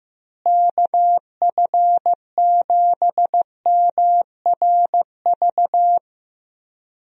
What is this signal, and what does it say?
Morse code, heard as a keyed tone, about 15 wpm, 700 Hz. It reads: KF7MRV
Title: USA Call Signs - 6 Characters - Repeated 2x - 15wpm